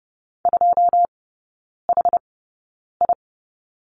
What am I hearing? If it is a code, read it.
Morse code, heard as a keyed tone, about 30 wpm, spaced as for 9 wpm, 700 Hz. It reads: 2HI
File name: Lesson 16e - Letter c - Three Characters 30wpm